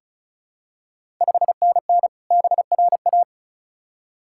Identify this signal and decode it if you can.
Morse code, heard as a keyed tone, about 36 wpm, 700 Hz. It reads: 5NN BRA
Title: Ontario QSO Party Exchange 35wpm